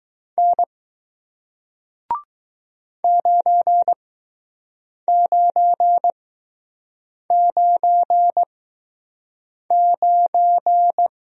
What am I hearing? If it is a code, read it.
Morse code, heard as a keyed tone, about 20 wpm, spaced as for 6 wpm, 700 Hz. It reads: N 9999